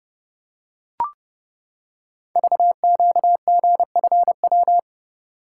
Morse code, heard as a keyed tone, about 29 wpm, 700 Hz. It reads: VQGFW
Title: Code Groups - 5 Letters - 30wpm